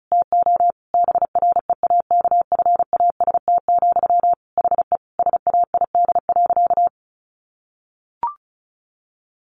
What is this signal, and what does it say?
Morse code, heard as a keyed tone, about 34 wpm, 700 Hz. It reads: TO BREAKFAST, HE SAID.
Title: Sentences from Top 1500 Words 35wpm